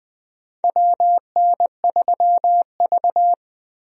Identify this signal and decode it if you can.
Morse code, heard as a keyed tone, about 20 wpm, 700 Hz. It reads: WN3V
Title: USA Call Signs - 4 Characters - Repeated 3x - 20wpm